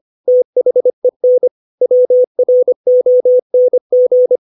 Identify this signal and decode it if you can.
Morse code, heard as a keyed tone, about 25 wpm, 500 Hz. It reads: THEN WRONG